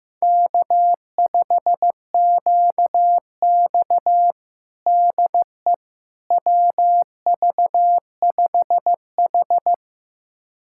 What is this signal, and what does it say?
Morse code, heard as a keyed tone, about 15 wpm, 700 Hz. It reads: K5QX DE WV5H
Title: USA Callsign DE Callsign - 4 Characters - 15wpm